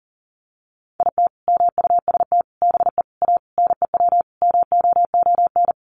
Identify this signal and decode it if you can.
Morse code, heard as a keyed tone, about 40 wpm, 700 Hz. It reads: IT MUST BE A NEW MOON
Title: Sentences from Top 500 Words 40wpm